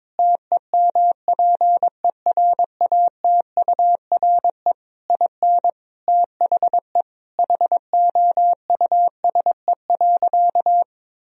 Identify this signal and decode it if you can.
Morse code, heard as a keyed tone, about 22 wpm, 700 Hz. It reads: TEMPERATURE IN THE HOUSE.
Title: Sentences from Top 1000 Words - Review Part 4 - 22wpm